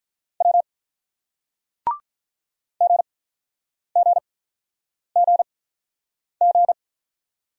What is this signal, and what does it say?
Morse code, heard as a keyed tone, about 44 wpm, spaced as for 9 wpm, 700 Hz. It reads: W GGGG
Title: Speed-Racing Single Letters 35wpm